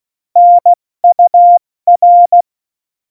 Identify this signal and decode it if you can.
Morse code, heard as a keyed tone, about 16 wpm, 700 Hz. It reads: NUR
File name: Morsekurs / Witze / Witz 19. November 2019, 70 F80